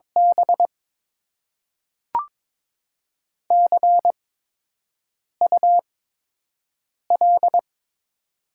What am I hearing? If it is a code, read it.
Morse code, heard as a keyed tone, about 22 wpm, spaced as for 6 wpm, 700 Hz. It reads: B CUL
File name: Top 90 QSO Elements - 8x Character Spacing - 22wpm